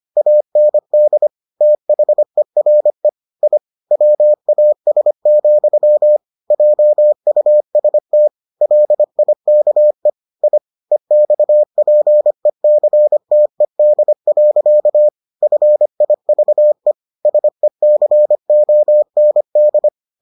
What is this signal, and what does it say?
Morse code, heard as a keyed tone, about 25 wpm, 600 Hz. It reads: AND THERE I WAS, JUST LIKE I EXPECTED. FIVE SECOND